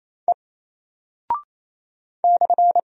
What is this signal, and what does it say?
Morse code, heard as a keyed tone, about 27 wpm, 700 Hz. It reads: E /
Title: Single Character (Letter:Number:Punctuation:Prosign) 28wpm